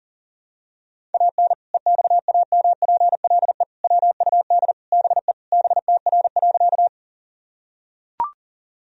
Morse code, heard as a keyed tone, about 40 wpm, 700 Hz. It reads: AN EXAMPLE WUD BE BTR.
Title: Sentences from Top 400 Words and Abbreviations 40wpm